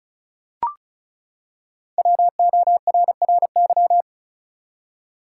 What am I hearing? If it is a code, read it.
Morse code, heard as a keyed tone, about 36 wpm, 700 Hz. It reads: WORRY
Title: Five Letter Words - 35wpm